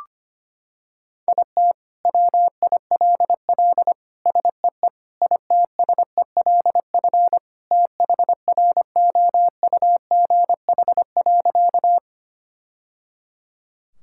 Morse code, heard as a keyed tone, about 25 wpm, 700 Hz. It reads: IT WILL SEE ITSELF THROUGH.
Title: Sentences from Top 1000 Words - Review Part 5 - 25wpm